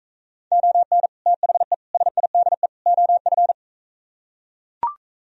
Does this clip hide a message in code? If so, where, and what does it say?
Yes: Morse code from 0.5 to 3.5 s, heard as a keyed tone, about 44 wpm, 700 Hz. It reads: ON THE SIDE OF